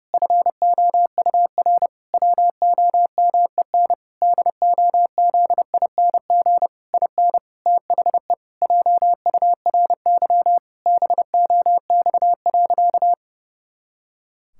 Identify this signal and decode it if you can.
Morse code, heard as a keyed tone, about 30 wpm, 700 Hz. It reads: FOUR WOMEN DOZING IN THE JURY BOX.
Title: Pangram Challenge v2 - 30wpm